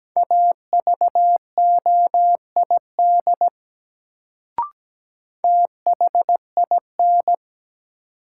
Morse code, heard as a keyed tone, about 17 wpm, 700 Hz. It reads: AVOID THIN
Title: Sentences from Top 1200 Words - New Word Drill Alt - 17wpm